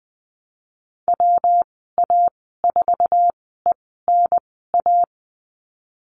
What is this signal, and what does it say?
Morse code, heard as a keyed tone, about 20 wpm, spaced as for 15 wpm, 700 Hz. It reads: WA4ENA
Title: 173 Lesson 28g - Number 9 - Call Signs 20wpm